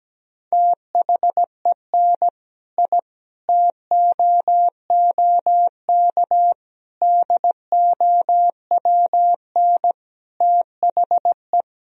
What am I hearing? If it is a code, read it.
Morse code, heard as a keyed tone, about 17 wpm, 700 Hz. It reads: THEN I TOOK DOWN THE
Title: Sentences from Top 1000 Words - Review Part 1 - 17wpm